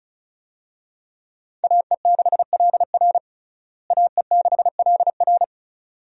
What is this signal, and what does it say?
Morse code, heard as a keyed tone, about 34 wpm, 700 Hz. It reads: AE6LR AE6LR
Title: USA Call Signs - 5 Characters - Repeated 2x - 35wpm